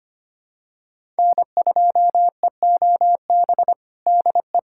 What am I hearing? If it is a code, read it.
Morse code, heard as a keyed tone, about 25 wpm, 700 Hz. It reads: N2EOB DE
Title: USA Callsign DE Callsign - 5 Characters - 25wpm